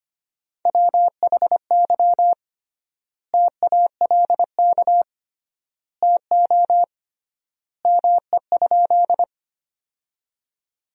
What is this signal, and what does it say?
Morse code, heard as a keyed tone, about 25 wpm, 700 Hz. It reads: WHY TALK TO ME?
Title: 3x Word-Spacing - Sentences from Top 400 Words - 25wpm